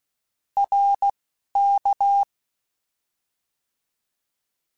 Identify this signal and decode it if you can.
Morse code, heard as a keyed tone, about 16 wpm, 800 Hz. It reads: R K
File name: Contrasts / Contrasts-mp3s / 2LtrContrasts @10